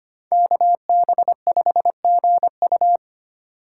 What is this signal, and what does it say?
Morse code, heard as a keyed tone, about 25 wpm, 700 Hz. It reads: KB5GU
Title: USA Call Signs - 5 Characters - 2x Character Spacing - 25wpm